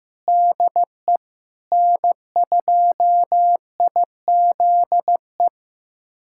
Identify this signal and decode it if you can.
Morse code, heard as a keyed tone, about 16 wpm, 700 Hz. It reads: DE N2IZE
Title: USA Callsign DE Callsign 15wpm